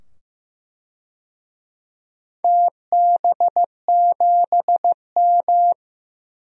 Morse code, heard as a keyed tone, about 15 wpm, 700 Hz. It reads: TB7M
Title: DXCC-100 Call Signs - 4 Characters - 15wpm